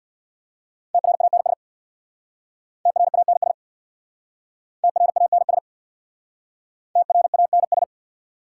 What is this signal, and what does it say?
Morse code, heard as a keyed tone, about 61 wpm, 690 Hz. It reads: TRANS TRANS TRANS TRANS
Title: Speed-Racing Prefixes and Suffixes 50wpm